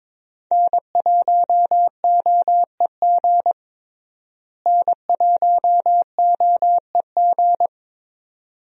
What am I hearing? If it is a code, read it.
Morse code, heard as a keyed tone, about 21 wpm, 700 Hz. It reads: N1OEG N1OEG
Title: USA Call Signs - 5 Characters - Repeated 2x - 22wpm